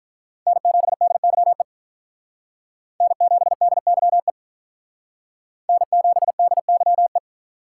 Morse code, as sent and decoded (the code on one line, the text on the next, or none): -. --... -.. -.-- . / -. --... -.. -.-- . / -. --... -.. -.-- .
N7DYE N7DYE N7DYE